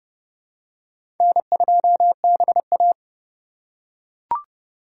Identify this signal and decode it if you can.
Morse code, heard as a keyed tone, about 30 wpm, 700 Hz. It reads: N2BA
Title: USA Call Signs - 4 Characters - 3x Character Spacing - 30wpm